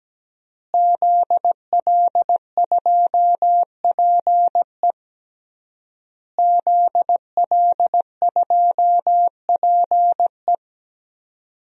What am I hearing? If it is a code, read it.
Morse code, heard as a keyed tone, about 17 wpm, 700 Hz. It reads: ZL2PE ZL2PE
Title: DXCC-100 Call Signs - 5 Characters - Repeated 3x - 17wpm